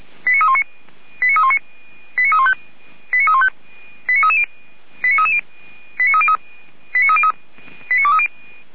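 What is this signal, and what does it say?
Five-tone selective call (ZVEI), heard as quick runs of five tones, tones of about 70 ms. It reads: 89319 89319 89315 89315 89330 89330 89303 89303 89230
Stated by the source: ZVEI five-tone selective call, received on SDR.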